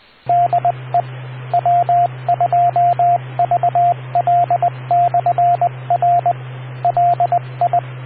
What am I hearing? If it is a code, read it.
Morse code, heard as a keyed tone, about 20 wpm, 700 Hz, with radio noise. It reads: DE W2VL/R LI